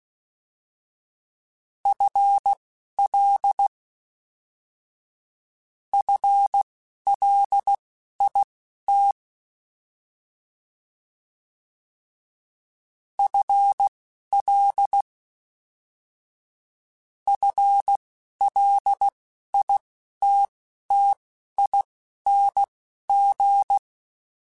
Morse code, heard as a keyed tone, about 16 wpm, spaced as for 11 wpm, 800 Hz. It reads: FL FLIT FL FLITTING
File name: FL / FL mp3s / FL Prefixes 1x1 @10